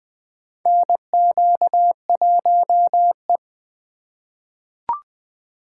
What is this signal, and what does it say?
Morse code, heard as a keyed tone, about 20 wpm, 700 Hz. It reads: NQ1E
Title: USA Call Signs - 4 Characters - 4x Character Spacing - 20wpm